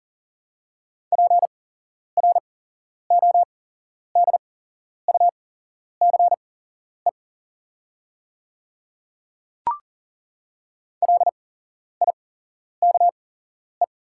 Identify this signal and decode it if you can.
Morse code, heard as a keyed tone, about 40 wpm, spaced as for 11 wpm, 690 Hz. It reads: PRODUCE LIKE
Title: 8x Character Spacing - Top 500 Words - 40wpm